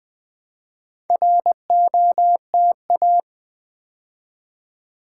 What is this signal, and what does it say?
Morse code, heard as a keyed tone, about 20 wpm, 700 Hz. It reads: ROTA
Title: Build A Word - 2 Letters At A Time - 6-Letter Words - 20wpm